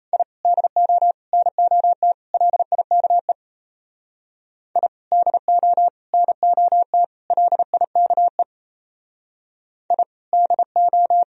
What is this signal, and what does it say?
Morse code, heard as a keyed tone, about 33 wpm, 700 Hz. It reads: I DO NOT LIKE I DO NOT LIKE I DO
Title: Speed-Racing - Sets of 4 Words 28wpm